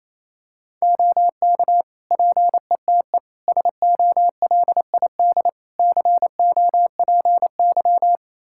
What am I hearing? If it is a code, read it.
Morse code, heard as a keyed tone, about 28 wpm, 700 Hz. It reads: OK PETE SOLID COPY
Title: Practice QSO - Exchange 3 - More Info - 28wpm